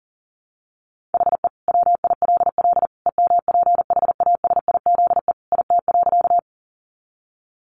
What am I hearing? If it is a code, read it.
Morse code, heard as a keyed tone, about 40 wpm, 700 Hz. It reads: HE WILL EMPHASIZE IT.
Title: Sentences from Top 1500 Words 40wpm